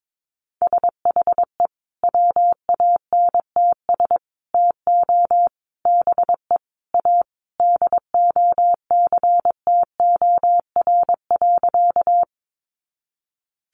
Morse code, heard as a keyed tone, about 22 wpm, 700 Hz. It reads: SHE WANTS TO BE A DOCTOR.
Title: Easy Phrases 22wpm